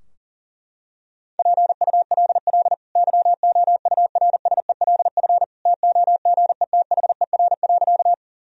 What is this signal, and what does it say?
Morse code, heard as a keyed tone, about 40 wpm, 700 Hz. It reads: PULL YOURSELF TOGETHER.
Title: Sentences from Top 1000 Words - Review Part 2 - 40wpm